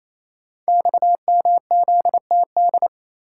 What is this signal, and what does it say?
Morse code, heard as a keyed tone, about 27 wpm, 700 Hz. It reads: XMZTD
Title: Code Groups - 5 Letter:Numbers - 28wpm